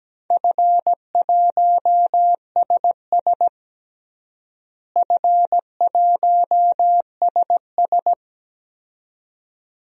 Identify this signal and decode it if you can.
Morse code, heard as a keyed tone, about 17 wpm, 700 Hz. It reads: F1SS F1SS
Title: DXCC-100 Call Signs - 4 Characters - Repeated 2x - 17wpm